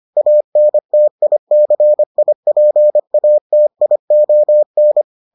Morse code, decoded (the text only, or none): ANTICIPATION